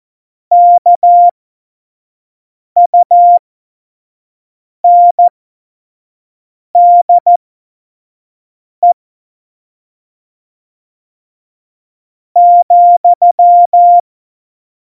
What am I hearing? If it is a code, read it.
Morse code, heard as a keyed tone, about 14 wpm, spaced as for 4 wpm, 700 Hz. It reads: KUNDE,
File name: Morsekurs / Witze / Witz 22. Januar 2019, 25 F70